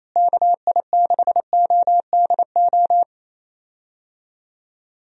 Morse code, heard as a keyed tone, about 28 wpm, 700 Hz. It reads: KI6ODO
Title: USA Call Signs - 6 Characters - Repeated 3x - 28wpm